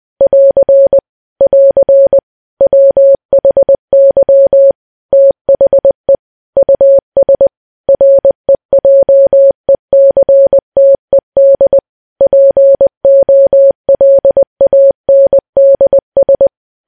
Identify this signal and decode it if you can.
Morse code, heard as a keyed tone, about 20 wpm, 550 Hz. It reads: <AR> <AR> WHY THE US REJECTED POLANDS
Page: news-headlines-in-morse-code-at-20-wpm